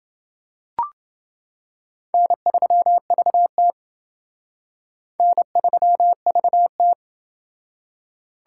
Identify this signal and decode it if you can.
Morse code, heard as a keyed tone, about 28 wpm, 700 Hz. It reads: N3VT N3VT